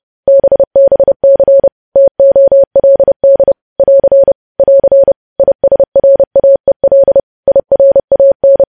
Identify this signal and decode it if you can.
Morse code, heard as a keyed tone, about 30 wpm, 550 Hz. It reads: BBC TOLD <AR> <AR> ISRAEL IRAN
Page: news headlines in morse code at 30 wpm